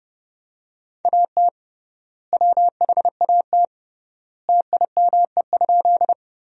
Morse code, heard as a keyed tone, about 30 wpm, 700 Hz. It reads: AT WHAT TIME?
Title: 3x Word-Spacing - Sentences from Top 100 Words - 30wpm